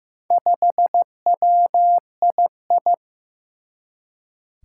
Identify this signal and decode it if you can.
Morse code, heard as a keyed tone, about 15 wpm, 700 Hz. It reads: 5WII
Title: Four Challenging Letter-Number Combinations 15wpm